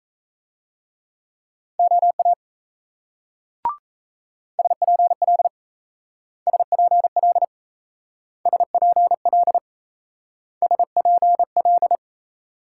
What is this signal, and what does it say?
Morse code, heard as a keyed tone, about 34 wpm, 700 Hz. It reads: OA SPL SPL SPL SPL
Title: Speed-Racing Multi-letter Phonemes - 28wpm